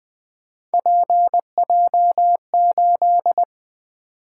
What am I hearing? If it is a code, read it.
Morse code, heard as a keyed tone, about 20 wpm, 700 Hz. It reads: PJ8
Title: Three Letter-Number Combinations - 20wpm